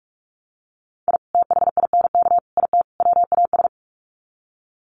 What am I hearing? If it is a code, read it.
Morse code, heard as a keyed tone, about 44 wpm, 700 Hz. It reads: I THINK IT WAS
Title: Sets of 4 Words 45wpm